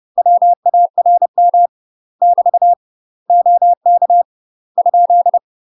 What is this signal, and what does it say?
Morse code, heard as a keyed tone, about 30 wpm, 700 Hz. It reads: WARM <BT> OK ?
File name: UK MASTER QSOs TRAINING / 30WPM / UK MASTER QSOs 0016